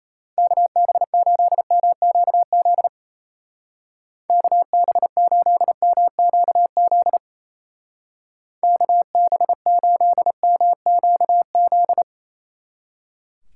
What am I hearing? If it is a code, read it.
Morse code, heard as a keyed tone, about 35 wpm, 700 Hz. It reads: KB8MQZ KB8MQZ KB8MQZ